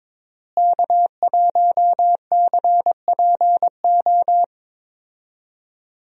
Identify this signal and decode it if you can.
Morse code, heard as a keyed tone, about 21 wpm, 700 Hz. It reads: K1CPO
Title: USA Call Signs - 5 Characters - Repeated 2x - 22wpm